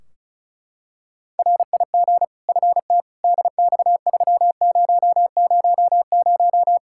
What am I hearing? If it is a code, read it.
Morse code, heard as a keyed tone, about 34 wpm, 700 Hz. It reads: RIG FT DX3000
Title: K1USN SST QSO Practice 35wpm